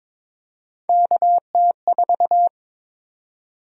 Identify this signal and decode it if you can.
Morse code, heard as a keyed tone, about 21 wpm, 700 Hz. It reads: KT4
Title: USA Call Sign Prefixes - 22wpm